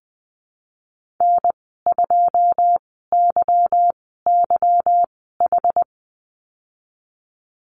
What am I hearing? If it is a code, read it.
Morse code, heard as a keyed tone, about 20 wpm, spaced as for 15 wpm, 700 Hz. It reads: N2YYH
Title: 173 Lesson 28g - Number 9 - Call Signs 20wpm